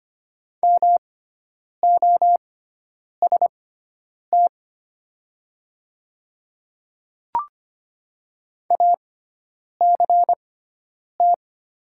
Morse code, heard as a keyed tone, about 25 wpm, spaced as for 9 wpm, 700 Hz. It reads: MOST ACT
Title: 6x Character Spacing - Top 500 Words - 25wpm